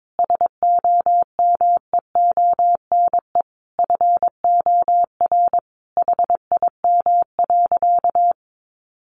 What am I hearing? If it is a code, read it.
Morse code, heard as a keyed tone, about 22 wpm, 700 Hz. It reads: SOMEONE FOR HIM.
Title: Sentences from Top 600 Words 22wpm